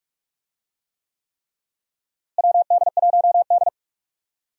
Morse code, heard as a keyed tone, about 46 wpm, 700 Hz. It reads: WD1D
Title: USA Call Signs - 4 Characters - Repeated 3x - 45wpm